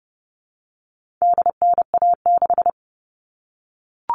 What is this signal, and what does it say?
Morse code, heard as a keyed tone, about 29 wpm, 700 Hz. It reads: DNA6
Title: All Cut Numbers - 4-Digit 30wpm